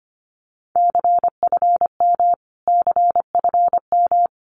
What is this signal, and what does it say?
Morse code, heard as a keyed tone, about 25 wpm, 700 Hz. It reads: CFM CFM
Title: Abbreviations Repeated 3x - 25wpm